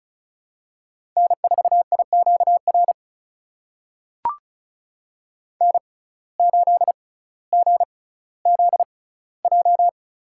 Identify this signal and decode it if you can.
Morse code, heard as a keyed tone, about 34 wpm, spaced as for 18 wpm, 700 Hz. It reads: N4IQR N8GZJ
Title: USA Call Signs - 5 Characters - 6x Character Spacing - 35wpm